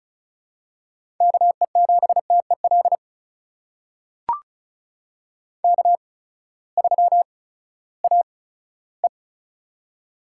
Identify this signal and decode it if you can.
Morse code, heard as a keyed tone, about 36 wpm, spaced as for 18 wpm, 700 Hz. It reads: KE7TEL K3AE
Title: USA Call Signs - 8x Character Spacing - 35wpm